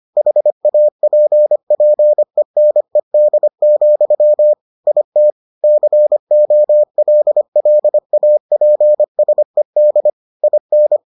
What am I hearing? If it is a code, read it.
Morse code, heard as a keyed tone, about 25 wpm, 600 Hz. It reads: HAPPENED, IT COLLAPSED IN